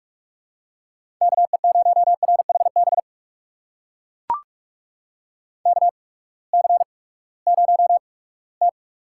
Morse code, heard as a keyed tone, about 43 wpm, spaced as for 22 wpm, 700 Hz. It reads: KE0RHB KC0T
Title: USA Call Signs - 6 Characters - 8x Character Spacing - 45wpm